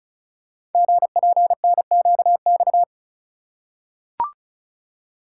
Morse code, heard as a keyed tone, about 36 wpm, 700 Hz. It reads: GPNQX
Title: Code Groups - 5 Letter:Numbers - 35wpm